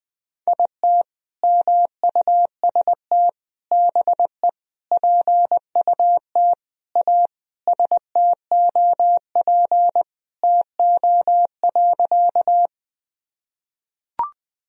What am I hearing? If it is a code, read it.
Morse code, heard as a keyed tone, about 20 wpm, 700 Hz. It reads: IT MUST BE PUT A STOP TO.
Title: Sentences from Top 1000 Words - Review Part 1 - 20wpm